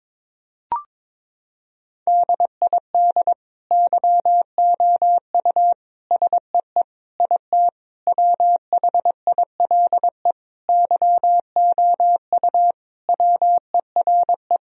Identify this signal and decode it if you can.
Morse code, heard as a keyed tone, about 21 wpm, 700 Hz. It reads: DID YOU SEE IT WHILE YOU WERE
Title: Sentences from Top 1000 Words - Review Part 1 - 22wpm